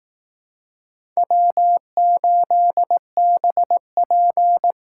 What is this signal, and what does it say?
Morse code, heard as a keyed tone, about 18 wpm, 700 Hz. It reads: W8BP
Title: Speed Racing - USA 1x2 Call Signs 15wpm